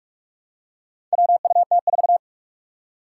Morse code, heard as a keyed tone, about 46 wpm, 700 Hz. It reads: WUT4
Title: Four Challenging Letter-Number Combinations 45wpm